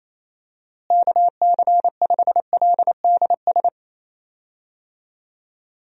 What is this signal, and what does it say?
Morse code, heard as a keyed tone, about 27 wpm, 700 Hz. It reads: KC5LDS
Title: USA Call Signs - 6 Characters - Repeated 3x - 28wpm